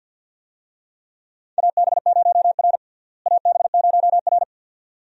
Morse code, heard as a keyed tone, about 50 wpm, 700 Hz. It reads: AB0R AB0R
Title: USA Call Signs - 4 Characters - Repeated 2x - 50wpm